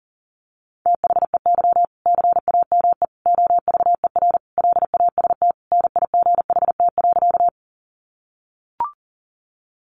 Morse code, heard as a keyed tone, about 40 wpm, 700 Hz. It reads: THEY CAME OVER LAST NIGHT.
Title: Sentences from Top 300 Words 40wpm